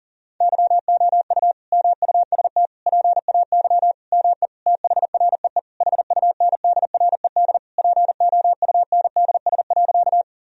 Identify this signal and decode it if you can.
Morse code, heard as a keyed tone, about 37 wpm, 700 Hz. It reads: YOU MUST PAY ME THREE HUNDRED POUNDS.
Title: Sentences from Top 1000 Words - Review Part 2 - 40wpm